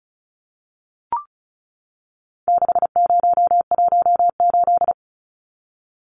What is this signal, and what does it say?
Morse code, heard as a keyed tone, about 34 wpm, 700 Hz. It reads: 6018
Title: Numbers - 4-Digit 35wpm